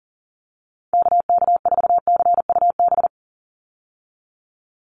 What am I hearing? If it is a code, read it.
Morse code, heard as a keyed tone, about 40 wpm, 700 Hz. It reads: KK4CUB
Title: USA Call Signs 40wpm